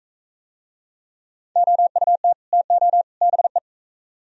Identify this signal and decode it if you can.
Morse code, heard as a keyed tone, about 39 wpm, 700 Hz. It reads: OUT TO BE